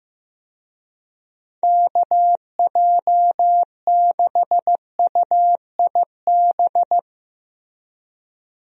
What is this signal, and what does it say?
Morse code, heard as a keyed tone, about 15 wpm, 700 Hz. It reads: KJ6UIB
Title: USA Call Signs - 6 Characters - Repeated 3x - 15wpm